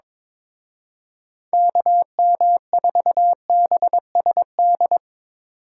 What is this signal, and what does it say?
Morse code, heard as a keyed tone, about 25 wpm, 700 Hz. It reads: KM4BSD
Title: Speed Racing - USA 2x3 Call Signs 22wpm